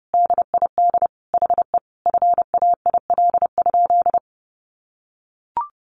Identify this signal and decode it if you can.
Morse code, heard as a keyed tone, about 30 wpm, 700 Hz. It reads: DID HE FAIL?
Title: Sentences from Top 1100 Words 30wpm